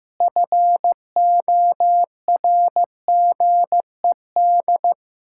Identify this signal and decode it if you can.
Morse code, heard as a keyed tone, about 15 wpm, 700 Hz. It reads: FORGED
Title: Pangram Challenge v2 - 15wpm